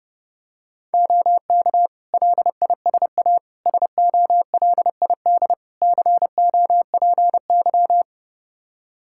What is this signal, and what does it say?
Morse code, heard as a keyed tone, about 30 wpm, 700 Hz. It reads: OK LISA SOLID COPY
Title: Practice QSO - Exchange 3 - More Info - 30wpm